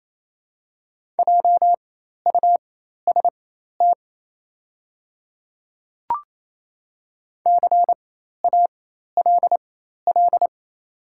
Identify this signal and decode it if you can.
Morse code, heard as a keyed tone, about 28 wpm, spaced as for 13 wpm, 700 Hz. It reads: JUST CALL
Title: 4x Character Spacing - Top 300 Words - 28wpm